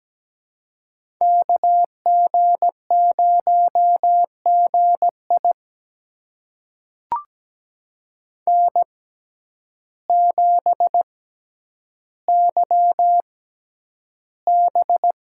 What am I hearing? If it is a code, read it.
Morse code, heard as a keyed tone, about 17 wpm, spaced as for 9 wpm, 700 Hz. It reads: KG0GI N7YB
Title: USA Call Signs - 5 Characters - 6x Character Spacing - 17wpm